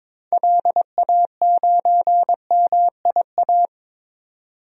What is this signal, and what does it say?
Morse code, heard as a keyed tone, about 22 wpm, 700 Hz. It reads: LA9MIA
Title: DXCC-100 Call Signs - 6 Characters - 22wpm